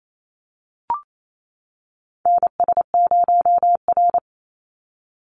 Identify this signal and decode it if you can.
Morse code, heard as a keyed tone, about 28 wpm, 700 Hz. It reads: NS0R
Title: Call Signs - 28wpm